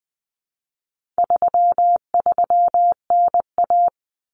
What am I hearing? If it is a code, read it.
Morse code, heard as a keyed tone, about 19 wpm, 700 Hz. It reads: 33NA
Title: Common Cut Numbers - 4-Digit 20wpm